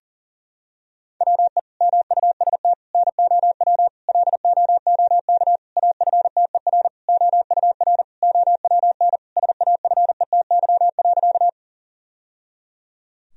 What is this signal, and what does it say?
Morse code, heard as a keyed tone, about 40 wpm, 700 Hz. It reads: WE MUST NOW LOOK AFTER OUR OWN SAFETY.
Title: Sentences from Top 1000 Words - Review Part 4 - 40wpm